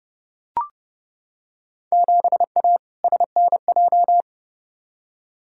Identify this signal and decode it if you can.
Morse code, heard as a keyed tone, about 30 wpm, 700 Hz. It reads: 7A SNJ
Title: Field Day Exchanges Encore - 30wpm